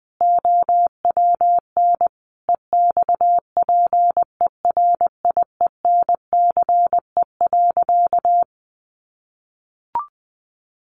Morse code, heard as a keyed tone, about 19 wpm, 700 Hz. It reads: OWN EXPERIENCE.
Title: Sentences from Top 600 Words 20wpm